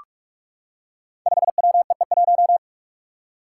Morse code, heard as a keyed tone, about 44 wpm, 700 Hz. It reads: 5WEE1
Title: Five Challenging Letter-Number Combinations 45wpm